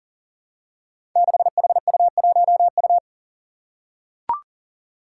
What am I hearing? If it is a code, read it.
Morse code, heard as a keyed tone, about 40 wpm, 700 Hz. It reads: 6HU1U
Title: Five Challenging Letter-Number Combinations 40wpm